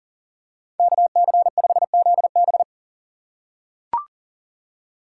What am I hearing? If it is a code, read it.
Morse code, heard as a keyed tone, about 40 wpm, 700 Hz. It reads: KC5ZB
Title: USA Call Signs - 4x Character Spacing - 40wpm